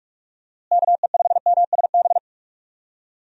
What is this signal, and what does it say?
Morse code, heard as a keyed tone, about 41 wpm, 700 Hz. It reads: KE5MSB
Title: USA Call Signs - 6 Characters - Repeated 3x - 45wpm